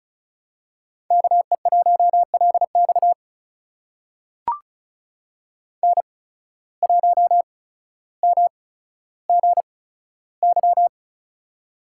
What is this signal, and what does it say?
Morse code, heard as a keyed tone, about 36 wpm, spaced as for 15 wpm, 700 Hz. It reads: KE1LX N1MGY
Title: USA Call Signs - 5 Characters - 8x Character Spacing - 35wpm